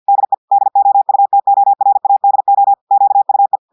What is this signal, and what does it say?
Morse code, heard as a keyed tone, about 50 wpm, 800 Hz. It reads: DE DOUTORADO QUE
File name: audios / portugais / portugais texte en clair 1 vitesse 50 mots minutes